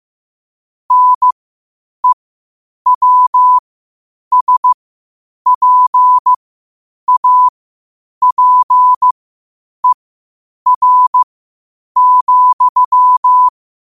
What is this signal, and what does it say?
Morse code, heard as a keyed tone, about 15 wpm, spaced as for 8 wpm, 1000 Hz. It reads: NEWSPAPER,
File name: qotdfiles / 8wpm